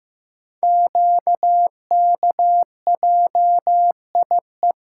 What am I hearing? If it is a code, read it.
Morse code, heard as a keyed tone, about 15 wpm, 700 Hz. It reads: QKJIE
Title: Code Groups - 5 Letters - 15wpm